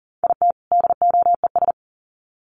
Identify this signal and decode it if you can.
Morse code, heard as a keyed tone, about 40 wpm, 700 Hz. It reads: IT DOES
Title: Sets of 2 Words 40wpm